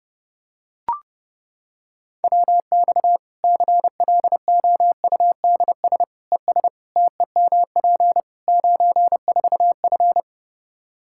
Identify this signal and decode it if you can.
Morse code, heard as a keyed tone, about 32 wpm, 700 Hz. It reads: WX CLOUDS ES TEMP 94F
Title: K1USN SST QSO Practice 30wpm